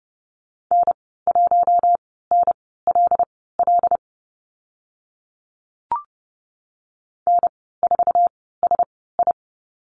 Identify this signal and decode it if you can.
Morse code, heard as a keyed tone, about 30 wpm, spaced as for 17 wpm, 700 Hz. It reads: N1NLL N4SI
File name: Lesson 15g - Number 5 - Call Signs 30wpm